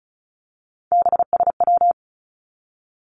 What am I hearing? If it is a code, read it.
Morse code, heard as a keyed tone, about 36 wpm, 700 Hz. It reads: BSW
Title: Three Challenging Letter-Number Combinations 35wpm